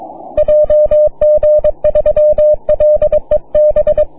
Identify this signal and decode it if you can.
Morse code, heard as a keyed tone, about 22 wpm, 600 Hz. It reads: JG3LEB
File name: wp / JG3LEB heard by LA8AJA on 6m